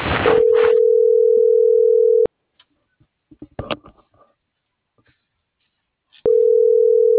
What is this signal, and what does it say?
Ringback tone (North America), 440 + 480 Hz.
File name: CENTRAL SAGOT / out anyelo1.sagot - .40945